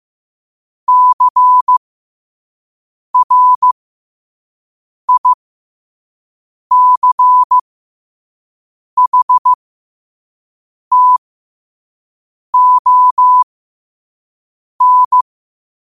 Morse code, heard as a keyed tone, about 15 wpm, spaced as for 5 wpm, 1000 Hz. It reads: CRICHTON
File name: qotdfiles / 5wpm